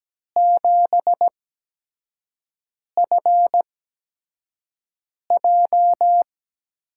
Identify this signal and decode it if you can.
Morse code, heard as a keyed tone, about 17 wpm, spaced as for 5 wpm, 700 Hz. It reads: 7FJ
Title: USA Call Signs - 6 Characters - 8x Character Spacing - 17wpm